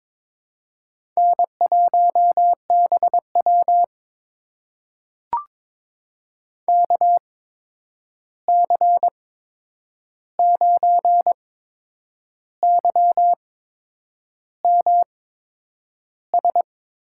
Morse code, heard as a keyed tone, about 22 wpm, spaced as for 8 wpm, 700 Hz. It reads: N1BW KC9YMS
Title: USA Call Signs - 8x Character Spacing - 22wpm